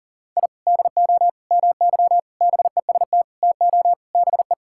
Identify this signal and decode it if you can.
Morse code, heard as a keyed tone, about 40 wpm, 700 Hz. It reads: I DO MY BEST TO BE
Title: Sentences from Top 1000 Words - Review Part 6 - 40wpm